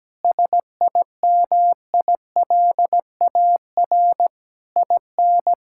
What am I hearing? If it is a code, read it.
Morse code, heard as a keyed tone, about 17 wpm, 700 Hz. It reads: SIMILAR IN
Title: Sentences from Top 1000 Words - Review Part 5 - 17wpm